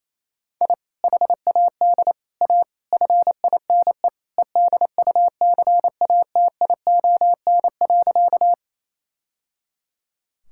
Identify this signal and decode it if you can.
Morse code, heard as a keyed tone, about 28 wpm, 700 Hz. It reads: I HAD A FINE EDUCATION.
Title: Sentences from Top 1000 Words - Review Part 6 - 28wpm